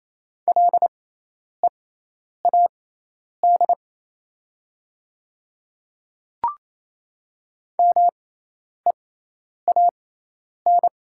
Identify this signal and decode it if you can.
Morse code, heard as a keyed tone, about 28 wpm, spaced as for 10 wpm, 700 Hz. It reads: LEAD MEAN
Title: 6x Character Spacing - Top 500 Words - 28wpm